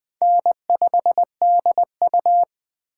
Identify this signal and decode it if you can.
Morse code, heard as a keyed tone, about 21 wpm, 700 Hz. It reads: N5DU
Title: Speed Racing - USA 1x2 Call Signs 17wpm